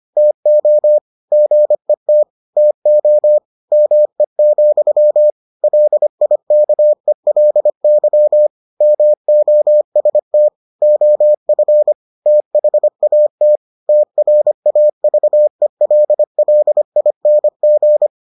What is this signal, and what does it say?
Morse code, heard as a keyed tone, about 25 wpm, 600 Hz. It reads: TO GET TO ME, LIKELY MOST OF THAT TRAVELLING